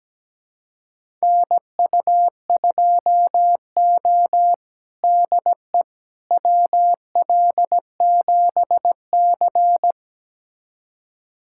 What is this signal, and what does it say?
Morse code, heard as a keyed tone, about 17 wpm, 700 Hz. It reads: NU2O DE WL7C
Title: USA Callsign DE Callsign - 4 Characters - 17wpm